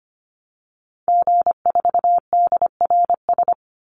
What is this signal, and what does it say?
Morse code, heard as a keyed tone, about 25 wpm, 700 Hz. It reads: G4DRS
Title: Call Signs - 25wpm